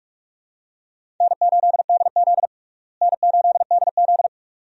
Morse code, heard as a keyed tone, about 46 wpm, 700 Hz. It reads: N8DZ N8DZ
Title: USA Call Signs - 4 Characters - Repeated 2x - 45wpm